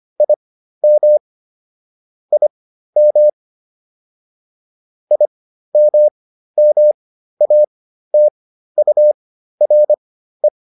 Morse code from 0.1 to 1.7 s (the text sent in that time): IM